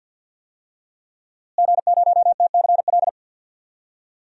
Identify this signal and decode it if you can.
Morse code, heard as a keyed tone, about 50 wpm, 700 Hz. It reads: G0TCL
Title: DXCC-100 Call Signs - 5 Characters - Repeated 2x - 50wpm